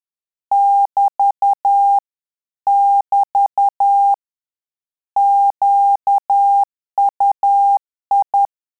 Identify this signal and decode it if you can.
Morse code, heard as a keyed tone, about 10 wpm, 800 Hz. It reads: <BT> <BT> QUI
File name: Telegraphy-group / QSO CW 10 PPM